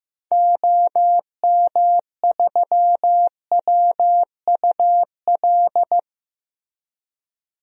Morse code, heard as a keyed tone, about 15 wpm, 700 Hz. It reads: OM3WUL
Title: DXCC-100 Call Signs - 6 Characters - Repeated 3x - 15wpm